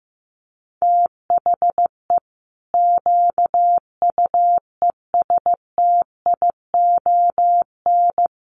Morse code, heard as a keyed tone, about 15 wpm, 700 Hz. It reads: THE QUESTION
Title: Sentences from Top 1200 Words 15wpm